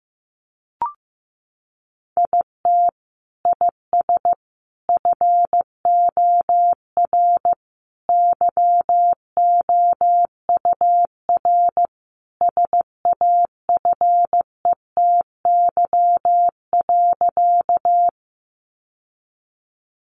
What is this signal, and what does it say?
Morse code, heard as a keyed tone, about 15 wpm, 700 Hz. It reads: IT IS FOR YOUR SAFETY.
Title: Sentences from Top 700 Words 15wpm